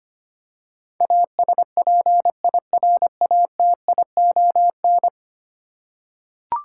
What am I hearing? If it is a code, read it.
Morse code, heard as a keyed tone, about 25 wpm, 700 Hz. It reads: ASPIRATION
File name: 10 Letter Words - First 3 Letters Given - 25wpm